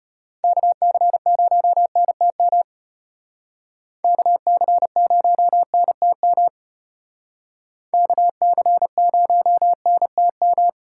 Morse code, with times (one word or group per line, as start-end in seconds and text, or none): 0.4-2.6 KC0NTM
4.0-6.5 KC0NTM
7.9-10.7 KC0NTM